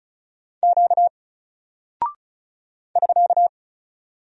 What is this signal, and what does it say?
Morse code, heard as a keyed tone, about 36 wpm, 700 Hz. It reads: Q <SK>
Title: Single Character (Letter:Number:Punctuation:Prosign) 35wpm